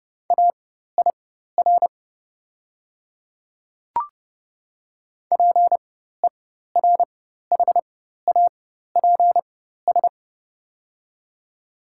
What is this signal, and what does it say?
Morse code, heard as a keyed tone, about 30 wpm, spaced as for 14 wpm, 700 Hz. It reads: AIR PERHAPS
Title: 4x Character Spacing - Top 500 Words - 30wpm